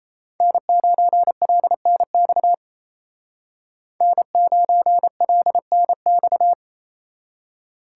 Morse code, heard as a keyed tone, about 37 wpm, 700 Hz. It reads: N9LNX N9LNX